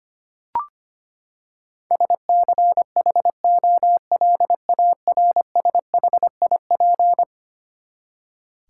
Morse code, heard as a keyed tone, about 25 wpm, 700 Hz. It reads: SCHOLARSHIP
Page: morse-code-ninja-practice-25wpm